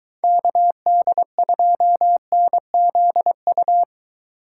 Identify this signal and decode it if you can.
Morse code, heard as a keyed tone, about 22 wpm, 700 Hz. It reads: KD2NZU